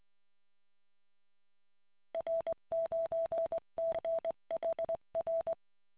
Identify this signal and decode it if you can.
Morse code, heard as a keyed tone, about 20 wpm, 660 Hz. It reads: R9CHR